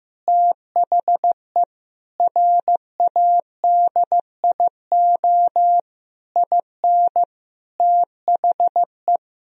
Morse code, heard as a keyed tone, about 16 wpm, 700 Hz. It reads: THE RADIO IN THE
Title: Sentences from Top 1000 Words - Review Part 3 - 15wpm